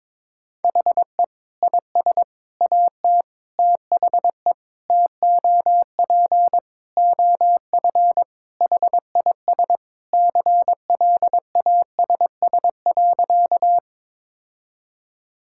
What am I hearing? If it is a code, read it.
Morse code, heard as a keyed tone, about 22 wpm, 700 Hz. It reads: HE IS AT THE TOP OF HIS CLASS.
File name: practice / 22wpm / Sentences with 4-Word Phrases - 22wpm